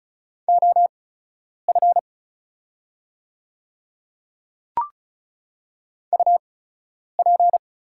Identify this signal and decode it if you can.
Morse code, heard as a keyed tone, about 36 wpm, spaced as for 10 wpm, 690 Hz. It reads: OF UP
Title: 8x Character Spacing - Top 100 Words - 35wpm